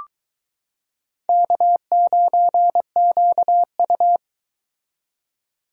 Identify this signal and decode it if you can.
Morse code, heard as a keyed tone, about 22 wpm, 700 Hz. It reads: K9QU